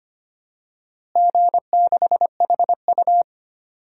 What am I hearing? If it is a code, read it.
Morse code, heard as a keyed tone, about 25 wpm, 700 Hz. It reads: G6HU
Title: DXCC-100 Call Signs - 4 Characters - Repeated 3x - 25wpm